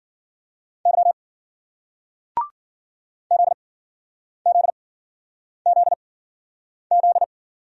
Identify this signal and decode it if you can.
Morse code, heard as a keyed tone, about 56 wpm, 700 Hz. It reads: Y Z Z Z Z